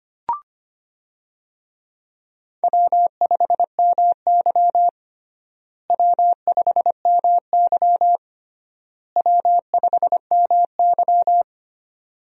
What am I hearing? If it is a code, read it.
Morse code, heard as a keyed tone, about 25 wpm, 700 Hz. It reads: W5MY W5MY W5MY
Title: USA Call Signs - 4 Characters - Repeated 3x - 25wpm